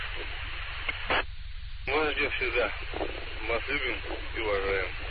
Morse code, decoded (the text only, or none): none